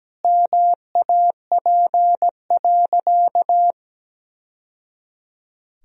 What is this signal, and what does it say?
Morse code, heard as a keyed tone, about 17 wpm, 700 Hz. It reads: MAP.
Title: Sentences from Top 1000 Words - Review Part 2 - 17wpm